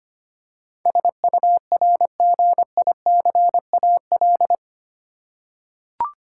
Morse code, heard as a keyed tone, about 25 wpm, 700 Hz. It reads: SURGICAL
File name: 8 Letter Words - First 3 Letters Given - 25wpm